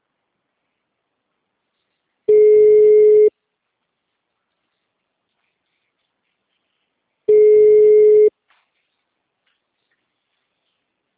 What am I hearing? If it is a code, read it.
Ringback tone (Europe, 425 Hz).
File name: out- -2010- - - .23010